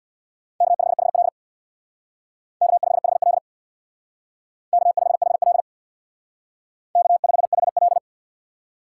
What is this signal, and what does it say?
Morse code, heard as a keyed tone, about 62 wpm, 700 Hz. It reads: K5HL K5HL K5HL K5HL
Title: Speed Racing - USA 1x2 Call Signs 50wpm